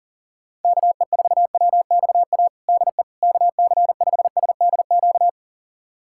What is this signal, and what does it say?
Morse code, heard as a keyed tone, about 40 wpm, 700 Hz. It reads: KE4WXA DE KC5SDQ
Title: USA Callsign DE Callsign - 6 Characters - 40wpm